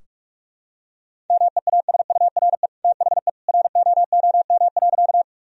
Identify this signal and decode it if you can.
Morse code, heard as a keyed tone, about 46 wpm, 700 Hz. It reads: MEASURE THE ROOM.
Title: Sentences from Top 1000 Words - Review Part 2 - 45wpm